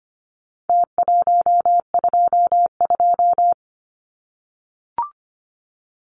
Morse code, heard as a keyed tone, about 25 wpm, 700 Hz. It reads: T122
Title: Numbers - Contest Style - 4-Digit 25wpm